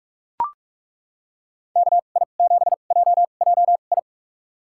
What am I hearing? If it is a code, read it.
Morse code, heard as a keyed tone, about 42 wpm, spaced as for 31 wpm, 700 Hz. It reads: KI7JJI
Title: USA Call Signs - 6 Characters - 2x Character Spacing - 45wpm